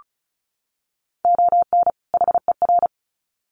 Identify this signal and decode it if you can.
Morse code, heard as a keyed tone, about 34 wpm, 700 Hz. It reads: ON HER
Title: Sets of 2 Words 35wpm